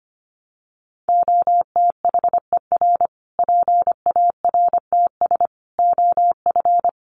Morse code, heard as a keyed tone, about 25 wpm, 700 Hz. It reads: OTHER PARTS OF
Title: Sets of 5 Words Encore 25wpm